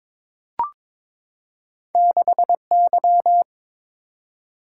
Morse code, heard as a keyed tone, about 22 wpm, 700 Hz. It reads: 6Y
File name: practice / 22wpm / Two Letter-Number Combinations - 22wpm